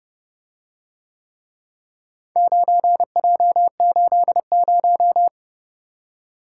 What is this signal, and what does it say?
Morse code, heard as a keyed tone, about 30 wpm, 700 Hz. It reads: 9J80